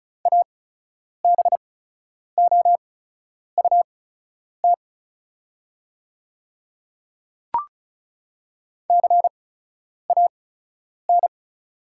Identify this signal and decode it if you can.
Morse code, heard as a keyed tone, about 36 wpm, spaced as for 10 wpm, 700 Hz. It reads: ABOUT CAN
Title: 8x Character Spacing - Top 100 Words - 35wpm